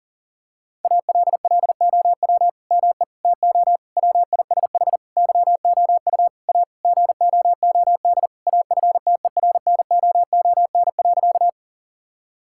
Morse code, heard as a keyed tone, about 40 wpm, 700 Hz. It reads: ALLOW ME TO WISH YOU A GOOD AFTERNOON.
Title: Sentences from Top 1600 Words 40wpm